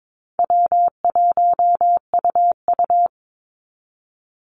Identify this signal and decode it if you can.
Morse code, heard as a keyed tone, about 22 wpm, 700 Hz. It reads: W1UU
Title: Call Signs Repeated 2x - 22wpm